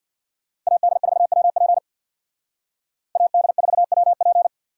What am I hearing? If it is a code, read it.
Morse code, heard as a keyed tone, about 63 wpm, 700 Hz. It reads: AD4WP AD4WP